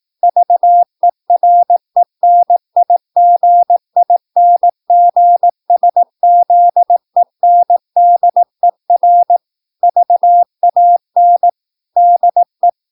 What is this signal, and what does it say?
Morse code, heard as a keyed tone, about 18 wpm, 700 Hz. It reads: VERENIGINGSZENDER VAN DE
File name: PI4AA Uitzending